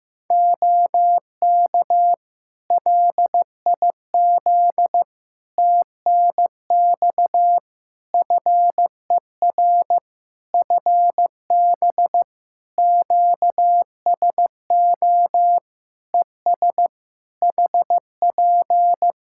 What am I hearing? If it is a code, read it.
Morse code, heard as a keyed tone, about 15 wpm, 700 Hz. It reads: OK LIZ TNX FER FB QSO ES HP
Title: K1USN SST QSO Practice 15wpm